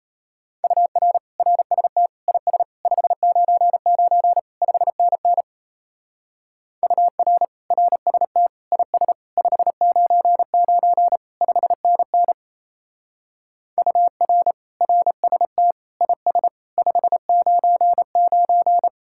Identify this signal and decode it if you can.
Morse code, heard as a keyed tone, about 33 wpm, 700 Hz. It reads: UR RST IS 599 5NN UR RST IS 599 5NN UR RST IS 599
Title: Speed-Racing Common QSO Phrases Encore - 28wpm